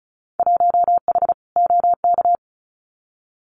Morse code, heard as a keyed tone, about 34 wpm, 690 Hz. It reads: 1H OK
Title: Winter Field Day Exchanges 35wpm